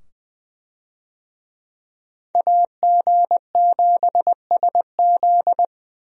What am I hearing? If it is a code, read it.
Morse code, heard as a keyed tone, about 20 wpm, 700 Hz. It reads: AG7SZ
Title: USA Call Signs - 5 Characters - 20wpm